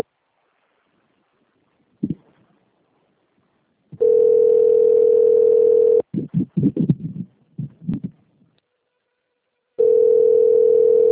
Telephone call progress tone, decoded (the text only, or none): ringback tone